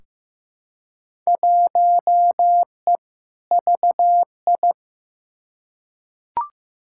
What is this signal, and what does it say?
Morse code, heard as a keyed tone, about 15 wpm, 700 Hz. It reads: 1E VI
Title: Field Day Exchanges Encore - 15wpm